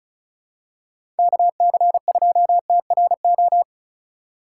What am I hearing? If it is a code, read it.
Morse code, heard as a keyed tone, about 34 wpm, 700 Hz. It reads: KC2TRO
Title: USA Call Signs - 6 Characters - 2x Character Spacing - 35wpm